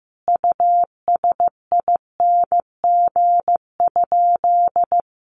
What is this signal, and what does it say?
Morse code, heard as a keyed tone, about 15 wpm, 700 Hz. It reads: USING?
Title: Common QSO Phrases 15wpm